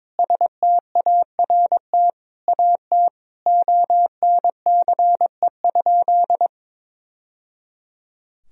Morse code, heard as a keyed tone, about 22 wpm, 700 Hz. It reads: START AT ONCE?
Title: Sentences from Top 1000 Words - Review Part 1 - 22wpm